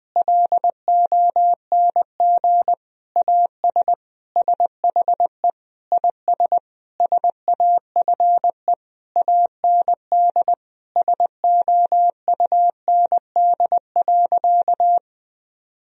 Morse code, heard as a keyed tone, about 20 wpm, 700 Hz. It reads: LONG AS SHE IS SAFE AND SOUND.
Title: Sentences from Top 1000 Words - Review Part 1 - 20wpm